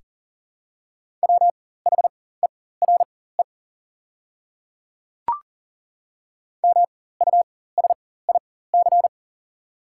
Morse code, heard as a keyed tone, about 40 wpm, spaced as for 19 wpm, 700 Hz. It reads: WHERE MUSIC
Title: 4x Character Spacing - Top 300 Words - 40wpm